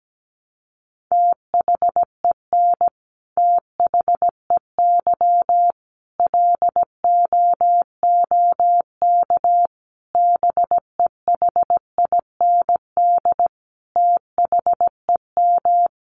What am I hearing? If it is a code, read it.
Morse code, heard as a keyed tone, about 17 wpm, 700 Hz. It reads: THEN THEY LOOK BEHIND THEM
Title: Sentences from Top 1300 Words 17wpm